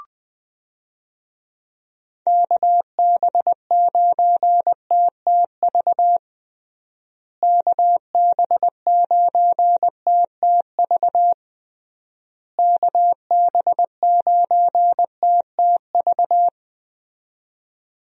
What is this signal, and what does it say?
Morse code, heard as a keyed tone, about 20 wpm, 700 Hz. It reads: KB9TTV KB9TTV KB9TTV
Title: USA Call Signs - 6 Characters - Repeated 3x - 20wpm